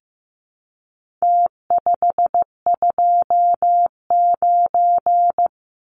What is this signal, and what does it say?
Morse code, heard as a keyed tone, about 15 wpm, 700 Hz. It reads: T529
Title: Numbers - Contest Style - 4-Digit 15wpm